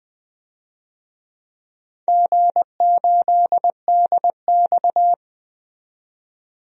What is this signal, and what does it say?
Morse code, heard as a keyed tone, about 21 wpm, 700 Hz. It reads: G8DX
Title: DXCC-100 Call Signs - 4 Characters - 20wpm